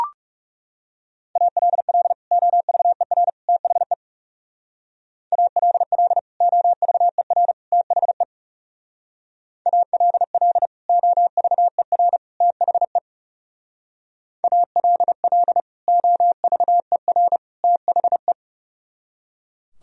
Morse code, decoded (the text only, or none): ALL OVER THE ALL OVER THE ALL OVER THE ALL OVER THE